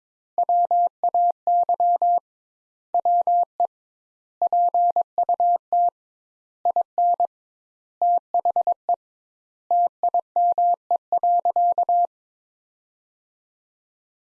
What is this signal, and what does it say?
Morse code, heard as a keyed tone, about 22 wpm, 700 Hz. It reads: WAY WE PUT IN THE TIME.
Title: Sentences Top 200 Words - 22wpm